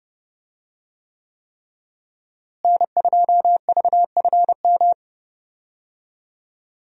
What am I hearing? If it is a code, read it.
Morse code, heard as a keyed tone, about 30 wpm, 700 Hz. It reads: N2VFM
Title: USA Call Signs - 5 Characters - Repeated 3x - 30wpm